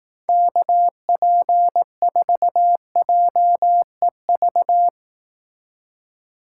Morse code, heard as a keyed tone, about 18 wpm, 700 Hz. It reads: KP4JEV
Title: Speed Racing - USA 2x3 Call Signs 15wpm